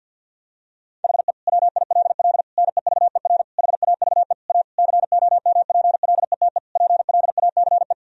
Morse code, heard as a keyed tone, about 52 wpm, 700 Hz. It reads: HE WILL NEVER HAVE A COMPLETE PLACE
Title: Sentences from Top 1000 Words - Review Part 2 - 50wpm